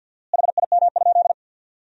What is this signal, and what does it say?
Morse code, heard as a keyed tone, about 51 wpm, 700 Hz. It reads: HIM?
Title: 4x Word-Spacing - Sentences from Top 200 Words - 50wpm